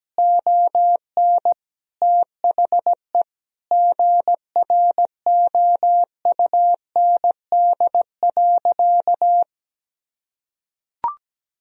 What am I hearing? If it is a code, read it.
Morse code, heard as a keyed tone, about 17 wpm, 700 Hz. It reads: ON THE GROUND.
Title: Sentences from Top 1000 Words - Review Part 2 - 17wpm